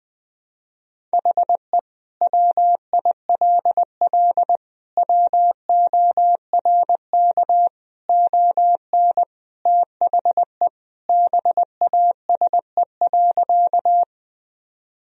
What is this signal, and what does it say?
Morse code, heard as a keyed tone, about 20 wpm, 700 Hz. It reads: HE WILL WORK ON THE BASE.
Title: Sentences from Top 1000 Words - Review Part 1 - 20wpm